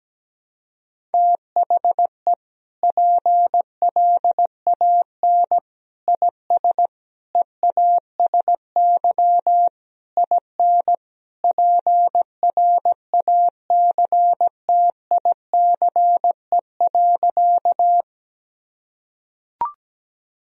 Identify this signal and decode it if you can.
Morse code, heard as a keyed tone, about 17 wpm, 700 Hz. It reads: THE PLAN IS EASY IN PRACTICE.
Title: Sentences from Top 1000 Words - Review Part 4 - 17wpm